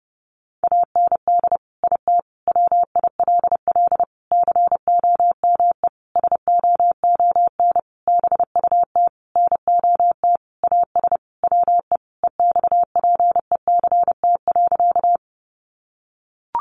Morse code, heard as a keyed tone, about 30 wpm, 700 Hz. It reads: AND IT WILL COME SOON BUT NOT AS WE EXPECT.
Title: Sentences from Top 1100 Words 30wpm